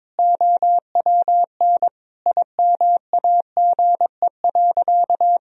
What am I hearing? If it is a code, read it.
Morse code, heard as a keyed tone, about 22 wpm, 700 Hz. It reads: OWN IMAGE.
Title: Sentences from Top 1000 Words - Review Part 4 - 22wpm